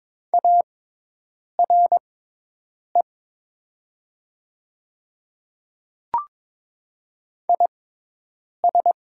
Morse code, heard as a keyed tone, about 22 wpm, spaced as for 8 wpm, 700 Hz. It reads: ARE IS
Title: 6x Character Spacing - Top 200 Words - 22wpm